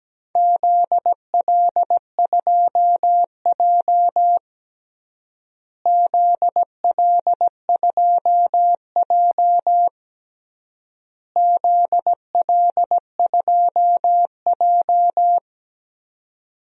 Morse code, heard as a keyed tone, about 17 wpm, 700 Hz. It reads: ZL2J ZL2J ZL2J
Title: DXCC-100 Call Signs - 4 Characters - Repeated 3x - 17wpm